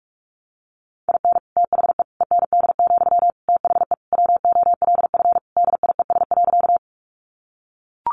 Morse code, heard as a keyed tone, about 44 wpm, 700 Hz. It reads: IN THE END, THE WOLF DIES.
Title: Additional Easy Phrases 45wpm